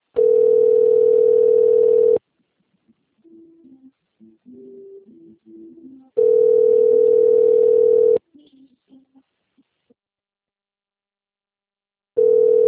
Ringback tone (North America), 440 + 480 Hz.
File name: rg-1600-unknown- - - .35591